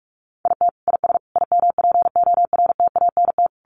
Morse code, heard as a keyed tone, about 44 wpm, 700 Hz. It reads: IT IS IMPORTANT
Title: Sentences from Top 1100 Words 45wpm